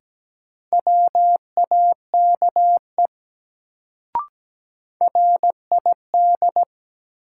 Morse code, heard as a keyed tone, about 17 wpm, 700 Hz. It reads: WAKE RID
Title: Sets of 4 Words - New Word Drill - 17wpm